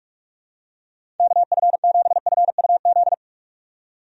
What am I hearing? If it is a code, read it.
Morse code, heard as a keyed tone, about 43 wpm, 700 Hz. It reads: KF7FUZ
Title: USA Call Signs - 6 Characters - 45wpm